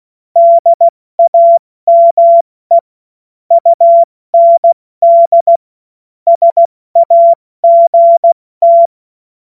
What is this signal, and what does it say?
Morse code, heard as a keyed tone, about 16 wpm, 680 Hz. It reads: DAME UND SAGT
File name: Morsekurs / Witze / Witz 23. Mai 2023 70 F80